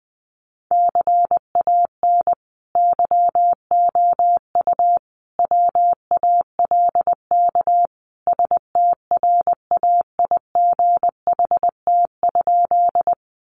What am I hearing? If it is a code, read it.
Morse code, heard as a keyed tone, about 20 wpm, 700 Hz. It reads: CAN YOU WALK STRAIGHT?
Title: Sentences from Top 1000 Words 20wpm